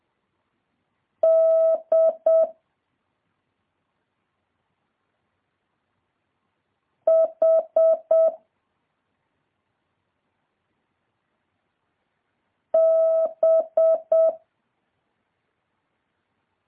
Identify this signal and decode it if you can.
Morse code, heard as a keyed tone, about 7 wpm, 650 Hz. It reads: D H B
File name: jh8cbh / 2023cw